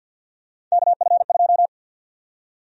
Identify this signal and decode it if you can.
Morse code, heard as a keyed tone, about 50 wpm, 700 Hz. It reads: KF2
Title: USA Call Sign Prefixes - 50wpm